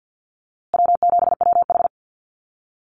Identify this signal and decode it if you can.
Morse code, heard as a keyed tone, about 48 wpm, 700 Hz. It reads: W7WH
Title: USA Call Signs Repeated 2x 50wpm